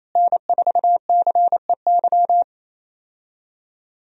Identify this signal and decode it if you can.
Morse code, heard as a keyed tone, about 28 wpm, 700 Hz. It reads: N4CEY
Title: USA Call Signs - 5 Characters - Repeated 2x - 28wpm